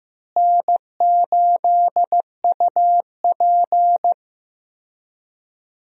Morse code, heard as a keyed tone, about 15 wpm, 700 Hz. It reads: N8UP
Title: USA Call Signs - 4 Characters - Repeated 3x - 15wpm